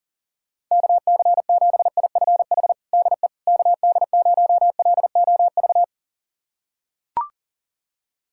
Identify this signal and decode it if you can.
Morse code, heard as a keyed tone, about 40 wpm, 700 Hz. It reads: KC7IFH DE KD0LOV
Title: USA Callsign DE Callsign 40wpm